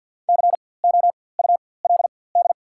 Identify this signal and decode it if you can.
Morse code, heard as a keyed tone, about 54 wpm, spaced as for 24 wpm, 690 Hz. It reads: COULD
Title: 4x Character Spacing - Top 300 Words - 50wpm